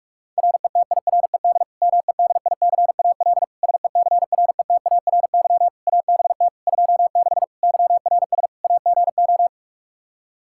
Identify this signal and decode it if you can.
Morse code, heard as a keyed tone, about 46 wpm, 700 Hz. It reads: RETIRED MEDICAL SECRETARY ABT 26 YRS AGO
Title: K1USN SST QSO Practice 45wpm